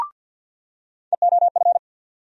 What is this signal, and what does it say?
Morse code, heard as a keyed tone, about 48 wpm, 700 Hz. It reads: EOF
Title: USA Call Sign Postfixes - 3 Letters 50wpm